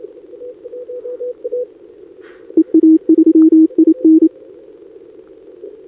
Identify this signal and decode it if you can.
Morse code, heard as a keyed tone, about 27 wpm, 320 Hz. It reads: EA3IN
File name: CQWWCW2025 / NedeleOdpol 20m Scan